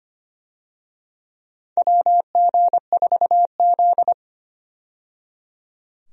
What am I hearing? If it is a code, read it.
Morse code, heard as a keyed tone, about 25 wpm, 700 Hz. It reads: WG4Z
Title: USA Call Signs - 4 Characters - 25wpm